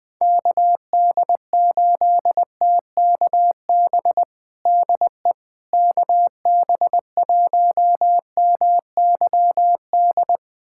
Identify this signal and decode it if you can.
Morse code, heard as a keyed tone, about 21 wpm, 700 Hz. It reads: KD8TKB DE KB1MYD
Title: USA Callsign DE Callsign - 6 Characters - 20wpm